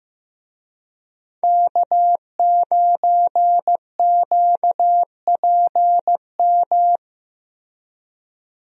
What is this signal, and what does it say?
Morse code, heard as a keyed tone, about 15 wpm, 700 Hz. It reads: K9QPM
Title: USA Call Signs - 5 Characters - Repeated 2x - 15wpm